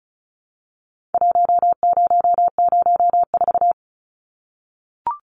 Morse code, heard as a keyed tone, about 34 wpm, 700 Hz. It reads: 1004
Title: Numbers - Contest Style - 4-Digit 35wpm